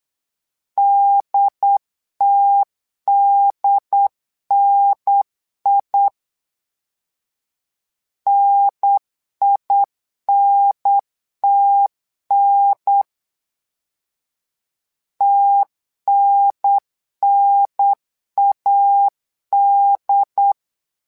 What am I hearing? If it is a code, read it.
Morse code, heard as a keyed tone, about 8 wpm, 800 Hz. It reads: DTDNI NINTN TNNAD
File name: LEZIONI / 2 UHDN / 2 UHDN 8wpm2232